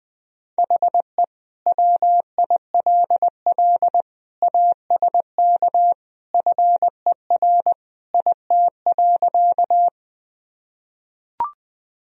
Morse code, heard as a keyed tone, about 20 wpm, 700 Hz. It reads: HE WILL ASK FER IT.
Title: Sentences from Top 200 Words and Abbreviations 20wpm